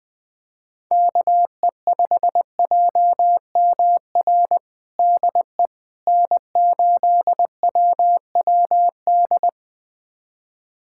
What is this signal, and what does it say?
Morse code, heard as a keyed tone, about 21 wpm, 700 Hz. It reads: KE5JMR DE N8WWD
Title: USA Callsign DE Callsign 20wpm